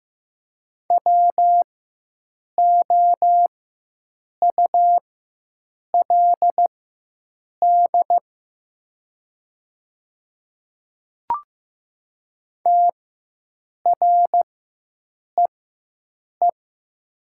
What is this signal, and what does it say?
Morse code, heard as a keyed tone, about 15 wpm, spaced as for 7 wpm, 700 Hz. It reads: WOULD TREE
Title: 4x Character Spacing - Top 300 Words - 15wpm